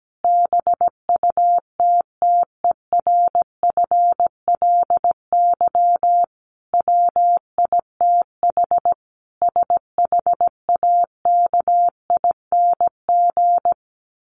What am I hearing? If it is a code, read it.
Morse code, heard as a keyed tone, about 17 wpm, 700 Hz. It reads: BUTTERFLY WITH SHAKING
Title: A Sound Of Thunder 17wpm